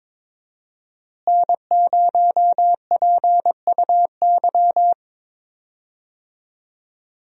Morse code, heard as a keyed tone, about 22 wpm, 700 Hz. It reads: N0PUY
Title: USA Call Signs - 5 Characters - Repeated 2x - 22wpm